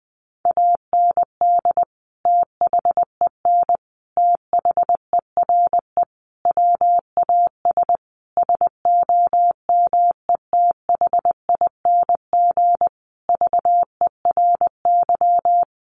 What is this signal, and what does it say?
Morse code, heard as a keyed tone, about 20 wpm, 700 Hz. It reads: AND THEN THERE WAS SOMETHING VERY
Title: Sentences from Top 1400 Words 20wpm